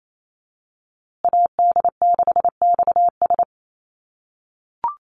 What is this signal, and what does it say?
Morse code, heard as a keyed tone, about 28 wpm, 700 Hz. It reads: AD6XS
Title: USA Call Signs Repeated 2x 28wpm